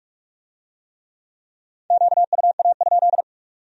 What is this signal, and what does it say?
Morse code, heard as a keyed tone, about 41 wpm, 700 Hz. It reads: QUA?